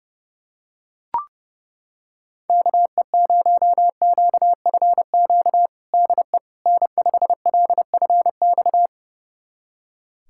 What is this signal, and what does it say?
Morse code, heard as a keyed tone, about 32 wpm, 700 Hz. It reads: KE0QFQ DE N5LFX
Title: USA Callsign DE Callsign 30wpm